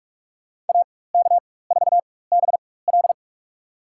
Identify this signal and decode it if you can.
Morse code, heard as a keyed tone, about 47 wpm, spaced as for 21 wpm, 700 Hz. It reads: AK4BL
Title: USA Call Signs - 5 Characters - 4x Character Spacing - 45wpm